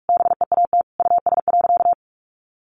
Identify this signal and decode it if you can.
Morse code, heard as a keyed tone, about 44 wpm, 700 Hz. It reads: BEAT US.
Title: Sentences from Top 1300 Words 45wpm